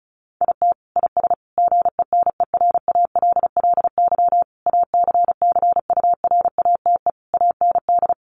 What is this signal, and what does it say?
Morse code, heard as a keyed tone, about 34 wpm, 700 Hz. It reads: IT IS GENERALLY ACCURATE AND
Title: Sentences from Top 1200 Words 35wpm